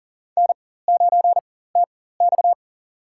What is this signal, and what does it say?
Morse code, heard as a keyed tone, about 40 wpm, spaced as for 19 wpm, 700 Hz. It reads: N9TX
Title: USA Call Signs - 4 Characters - 4x Character Spacing - 40wpm